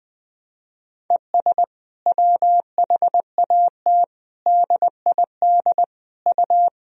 Morse code, heard as a keyed tone, about 21 wpm, 700 Hz. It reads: ES WHAT DID U
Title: Sentences from Top 100 Words and Abbreviations 20wpm